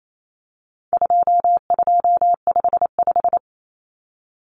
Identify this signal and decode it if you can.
Morse code, heard as a keyed tone, about 28 wpm, 700 Hz. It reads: 2255
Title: Numbers - 4-Digit 28wpm